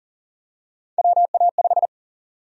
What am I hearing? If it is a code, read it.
Morse code, heard as a keyed tone, about 40 wpm, 700 Hz. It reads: WA5
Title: USA Call Sign Prefixes - 40wpm